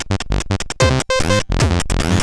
There is no code.